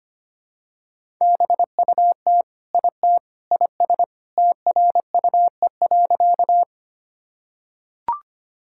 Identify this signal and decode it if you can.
Morse code, heard as a keyed tone, about 25 wpm, 700 Hz. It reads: BUT IT IS TRUE.
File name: practice / 25wpm / Sentences with 3-Word Phrases - 25wpm